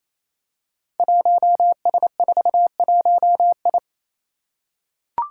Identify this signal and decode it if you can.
Morse code, heard as a keyed tone, about 29 wpm, 700 Hz. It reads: 1S41I
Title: Five Challenging Letter-Number Combinations 28wpm